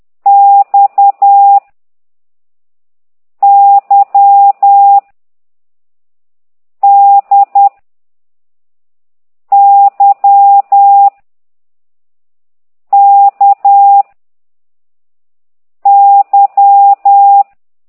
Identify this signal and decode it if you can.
Morse code, heard as a keyed tone, about 10 wpm, 800 Hz. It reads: X Y D Y K Y